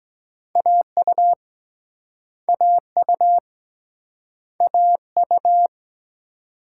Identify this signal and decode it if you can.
Morse code, heard as a keyed tone, about 25 wpm, 700 Hz. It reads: AU AU AU